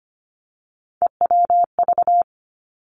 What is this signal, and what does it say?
Morse code, heard as a keyed tone, about 25 wpm, 700 Hz. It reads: EWV
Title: Three Challenging Letter-Number Combinations 25wpm